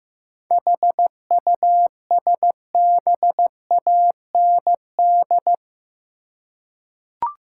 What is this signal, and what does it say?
Morse code, heard as a keyed tone, about 15 wpm, 700 Hz. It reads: HUSBAND
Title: Seven Letter Words - 15wpm